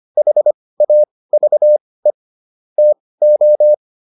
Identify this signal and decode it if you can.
Morse code, heard as a keyed tone, about 25 wpm, spaced as for 18 wpm, 600 Hz. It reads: HAVE TO
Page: intermediate-practice-files